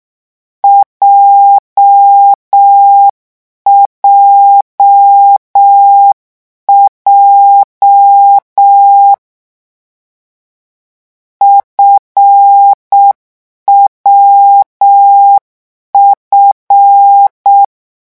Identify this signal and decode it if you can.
Morse code, heard as a keyed tone, about 6 wpm, 800 Hz. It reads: JJJ FWF
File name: morse mp3 / lecon7-vitesse 6